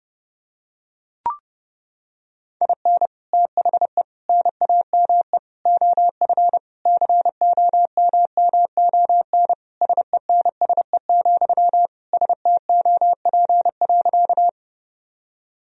Morse code, heard as a keyed tone, about 32 wpm, 700 Hz. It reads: IN THE NAME OF COMMON SENSE, STOP.
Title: Sentences from Top 1000 Words - Review Part 4 - 30wpm